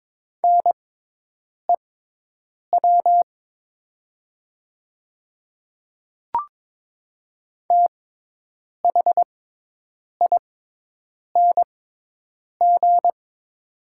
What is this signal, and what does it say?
Morse code, heard as a keyed tone, about 22 wpm, spaced as for 8 wpm, 700 Hz. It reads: NEW THING
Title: 6x Character Spacing - Top 300 Words - 22wpm